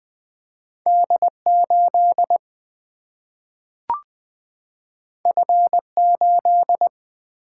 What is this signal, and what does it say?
Morse code, heard as a keyed tone, about 20 wpm, 700 Hz. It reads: D8 F8
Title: Single-Letter Single-Number 20wpm